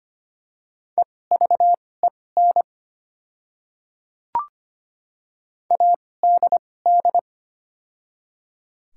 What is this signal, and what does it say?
Morse code, heard as a keyed tone, about 25 wpm, spaced as for 18 wpm, 700 Hz. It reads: EVEN ADD
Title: 2x Character Spacing - Top 300 Words - 25wpm